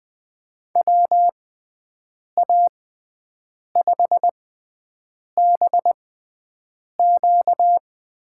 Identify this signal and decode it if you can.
Morse code, heard as a keyed tone, about 20 wpm, spaced as for 7 wpm, 700 Hz. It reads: WA5BQ
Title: USA Call Signs - 6 Characters - 6x Character Spacing - 20wpm